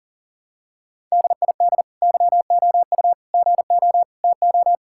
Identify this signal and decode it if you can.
Morse code, heard as a keyed tone, about 40 wpm, 700 Hz. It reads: DID YOU GO TO
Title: Sentences from Top 1000 Words - Review Part 5 - 40wpm